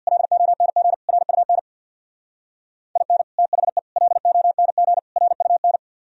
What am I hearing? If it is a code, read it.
Morse code, heard as a keyed tone, about 60 wpm, 700 Hz. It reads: LONG RUN IN THE LONG RUN
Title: Speed-Racing - Sets of 4 Words 50wpm